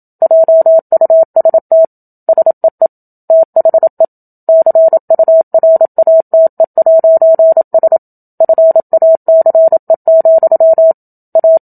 Morse code, heard as a keyed tone, about 27 wpm, 650 Hz. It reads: JUST SEE THE CURATE'S FACE, A
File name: Book Two / 01-Chapter 1 - 27 WPM